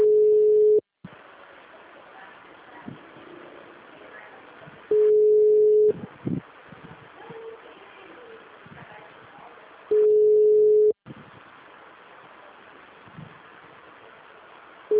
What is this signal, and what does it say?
Ringback tone (Europe, 425 Hz).
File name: out- -2002- - - .72791